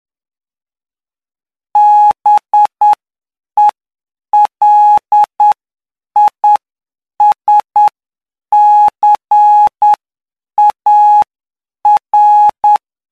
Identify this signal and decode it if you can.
Morse code, heard as a keyed tone, about 9 wpm, spaced as for 7 wpm, 840 Hz. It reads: BELISCAR
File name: cw / py2lcd / 11-PY2LCD-Li��o 20